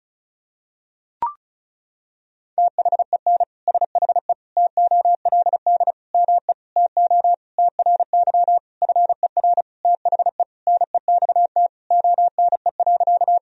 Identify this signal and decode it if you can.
Morse code, heard as a keyed tone, about 36 wpm, 700 Hz. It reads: THEN SHE TOLD ME TO TRY FER THE NEXT ONE.
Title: Sentences from Top 400 Words and Abbreviations 35wpm